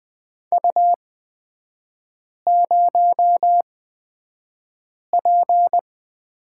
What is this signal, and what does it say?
Morse code, heard as a keyed tone, about 20 wpm, spaced as for 5 wpm, 700 Hz. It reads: U0P
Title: 250 Lesson 40e - BK - Three Characters 20wpm